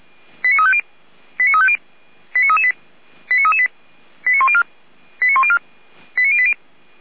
Five-tone selective call (ZVEI), heard as quick runs of five tones, tones of about 70 ms. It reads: 89350 89350 89338 89338 89104 89104 89080